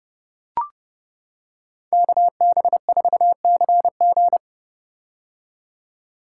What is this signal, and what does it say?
Morse code, heard as a keyed tone, about 30 wpm, 700 Hz. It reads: KB4CG
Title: K1USN SST QSO Practice - 30wpm